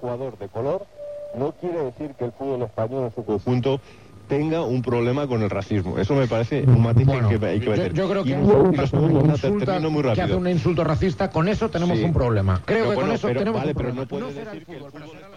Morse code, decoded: none